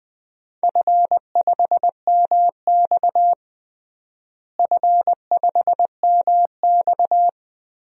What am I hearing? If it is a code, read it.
Morse code, heard as a keyed tone, about 21 wpm, 700 Hz. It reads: F5MX F5MX
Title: DXCC-100 Call Signs - 4 Characters - Repeated 3x - 20wpm